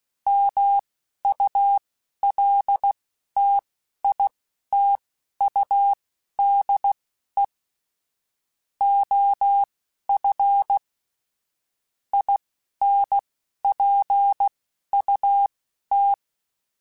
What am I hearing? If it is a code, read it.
Morse code, heard as a keyed tone, about 16 wpm, spaced as for 11 wpm, 800 Hz. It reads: MULTITUDE OF INPUT
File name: Technical Subjects / Technical Subjects-mp3s / OR Gate @10